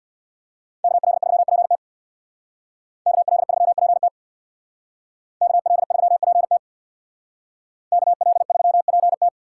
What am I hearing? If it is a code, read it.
Morse code, heard as a keyed tone, about 60 wpm, 700 Hz. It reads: KL3PT KL3PT KL3PT KL3PT
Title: Speed Racing - USA 2x2 Call Signs 50wpm